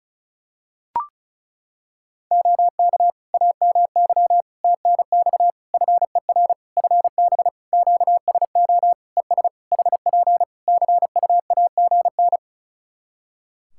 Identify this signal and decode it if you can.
Morse code, heard as a keyed tone, about 34 wpm, 700 Hz. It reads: OK AMY TNX FER FB QSO ES HP CUAGN
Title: Practice QSO - Exchange 4 - Ending - 35wpm